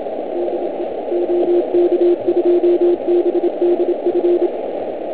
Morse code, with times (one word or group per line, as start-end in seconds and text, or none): 0.4-0.8 DE
1.1-4.5 OK2BDF